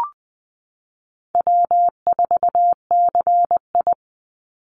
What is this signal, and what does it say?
Morse code, heard as a keyed tone, about 20 wpm, 700 Hz. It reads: W4CI
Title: Call Signs - 20wpm